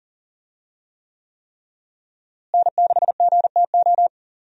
Morse code, heard as a keyed tone, about 40 wpm, 700 Hz. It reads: N6GTO
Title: USA Call Signs - 5 Characters - Repeated 3x - 40wpm